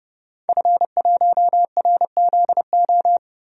Morse code, heard as a keyed tone, about 30 wpm, 700 Hz. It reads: F1RZO
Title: DXCC-100 Call Signs - 5 Characters - Repeated 2x - 30wpm